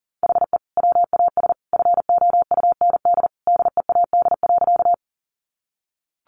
Morse code, heard as a keyed tone, about 40 wpm, 700 Hz. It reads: HE WAS FOUND DEAD.
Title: Sentences from Top 1500 Words 40wpm